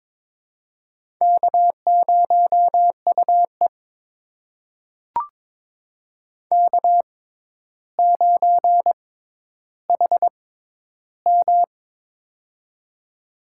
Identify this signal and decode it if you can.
Morse code, heard as a keyed tone, about 22 wpm, spaced as for 11 wpm, 700 Hz. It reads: K0UE K9HM
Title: USA Call Signs - 4 Characters - 6x Character Spacing - 22wpm